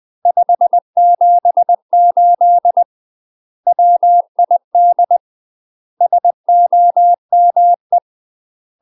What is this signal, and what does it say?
Morse code, heard as a keyed tone, about 20 wpm, 700 Hz. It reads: 578 WID SOME
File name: UK MASTER QSOs TRAINING / 20WPM / UK MASTER QSOs 0010